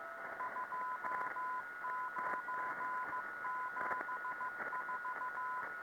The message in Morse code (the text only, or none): 2974